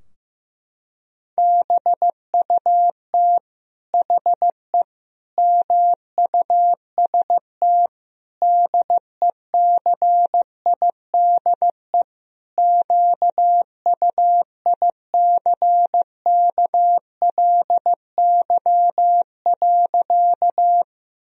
Morse code, heard as a keyed tone, about 15 wpm, 700 Hz. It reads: BUT HE MUST DECIDE QUICKLY.
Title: Sentences from Top 1000 Words - Review Part 3 - 15wpm